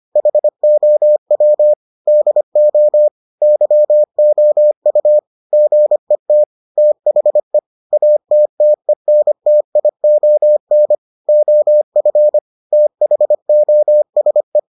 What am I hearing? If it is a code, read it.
Morse code, heard as a keyed tone, about 25 wpm, 600 Hz. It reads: HOW DO YOU GET THE ATTENTION OF THOSE